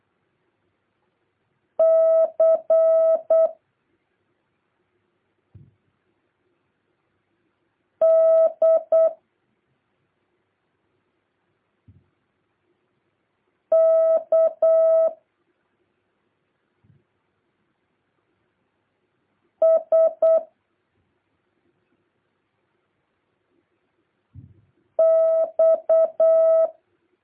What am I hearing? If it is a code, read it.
Morse code, heard as a keyed tone, about 8 wpm, 650 Hz. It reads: C D K S X
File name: jh8cbh / 2023cw